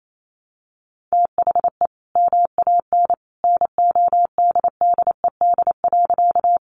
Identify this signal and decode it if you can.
Morse code, heard as a keyed tone, about 28 wpm, 700 Hz. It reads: THE MAN NODDED.
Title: A Sound Of Thunder 28wpm